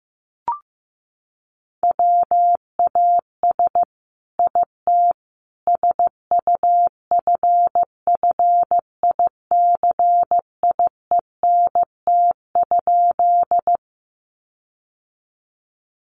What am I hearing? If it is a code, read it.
Morse code, heard as a keyed tone, about 15 wpm, 700 Hz. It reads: WAS IT SUFFICIENT?
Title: Sentences from Top 1400 Words 15wpm